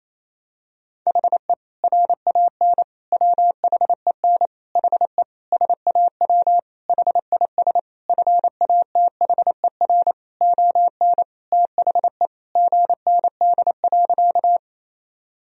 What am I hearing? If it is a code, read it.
Morse code, heard as a keyed tone, about 28 wpm, 700 Hz. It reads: HE RAN WHEN HE SAW HIS FATHER ON THE GND.
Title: Sentences from Top 500 Words and Abbreviations 28wpm